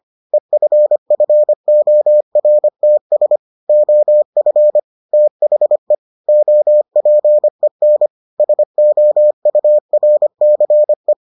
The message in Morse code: EFFORTS OF THE OPEN SOURCE

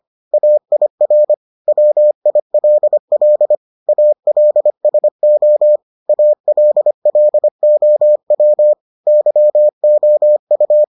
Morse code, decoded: AIR WILL ALSO ALLOW YOU

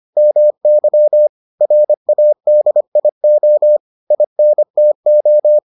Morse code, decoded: MY RADIO INTO

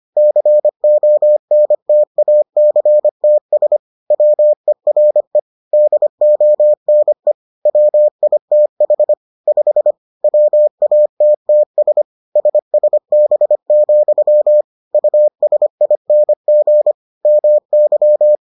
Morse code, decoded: CONTACTS WERE DONE WITH 5 WATTS SSB, USING MY

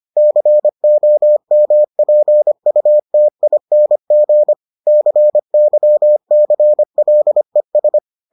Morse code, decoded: COMPUTING CYCLES

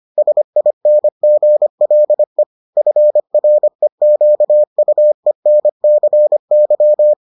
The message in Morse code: SINGLE FREQUENCY

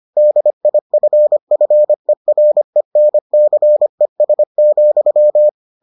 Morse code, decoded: DIFFERENCES,

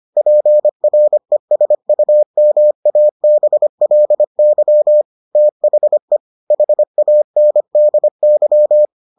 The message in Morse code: PRESUMABLY THE HANDY

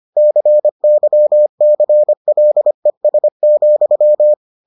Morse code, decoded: CYCLES,

